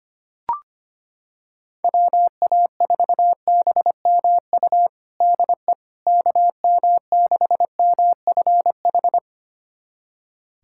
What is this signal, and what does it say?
Morse code, heard as a keyed tone, about 25 wpm, 700 Hz. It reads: WA4BMU DE KM6MFH
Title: USA Callsign DE Callsign 25wpm